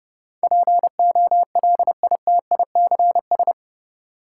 Morse code, heard as a keyed tone, about 31 wpm, 700 Hz. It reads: POLITICS
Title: Sentences from Top 1000 Words - New Word Drill - 30wpm